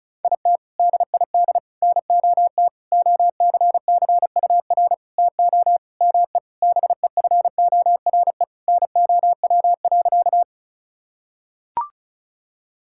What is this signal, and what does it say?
Morse code, heard as a keyed tone, about 34 wpm, 700 Hz. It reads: IT DID NOT OCCUR TO ME BEFORE NOW.
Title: Sentences from Top 1000 Words - Review Part 6 - 35wpm